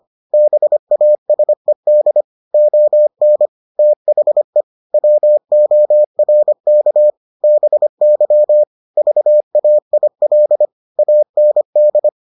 Morse code, heard as a keyed tone, about 25 wpm, 600 Hz. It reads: BASED ON THE WORK BY VAIL AND